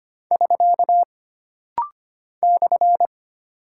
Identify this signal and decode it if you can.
Morse code, heard as a keyed tone, about 25 wpm, 700 Hz. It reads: <SK> /
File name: Single Punctuation-Prosign - Rapid-Fire - 25wpm